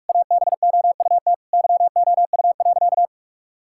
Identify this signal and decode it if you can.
Morse code, heard as a keyed tone, about 41 wpm, 700 Hz. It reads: ABOUT YOU.
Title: Sentences from Top 1000 Words - Review Part 6 - 45wpm